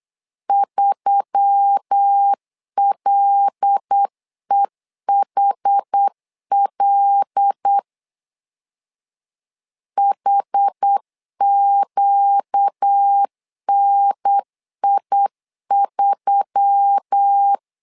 Morse code, heard as a keyed tone, about 8 wpm, 800 Hz. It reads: 3LEHL HQNI3
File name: LEZIONI / 5 LBVC / 5 L 8wpm 2831